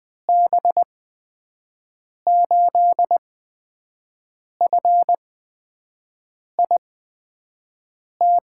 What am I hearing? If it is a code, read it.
Morse code, heard as a keyed tone, about 20 wpm, spaced as for 5 wpm, 700 Hz. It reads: B8FIT
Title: USA Call Signs - 8x Character Spacing - 20wpm